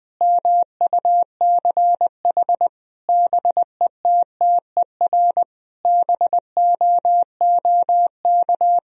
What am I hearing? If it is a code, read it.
Morse code, heard as a keyed tone, about 20 wpm, 700 Hz. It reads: MUCH BETTER BOOK
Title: Sentences from Top 1000 Words - Review Part 3 - 20wpm